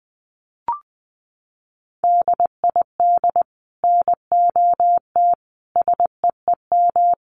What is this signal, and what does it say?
Morse code, heard as a keyed tone, about 20 wpm, 700 Hz. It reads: DID NOT SEEM
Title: Sets of 4 Words 20wpm